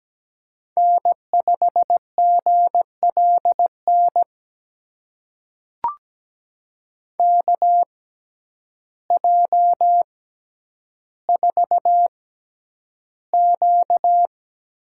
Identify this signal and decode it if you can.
Morse code, heard as a keyed tone, about 17 wpm, spaced as for 9 wpm, 700 Hz. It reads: N5GLN KJ4Q
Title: USA Call Signs - 5 Characters - 6x Character Spacing - 17wpm